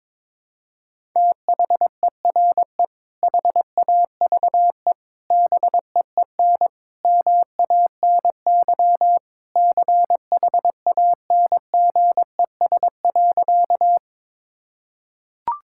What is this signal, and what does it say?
Morse code, heard as a keyed tone, about 22 wpm, 700 Hz. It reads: THERE HAVE BEEN MANY CHANGES.
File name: practice / 22wpm / Sentences with 3-Word Phrases - 22wpm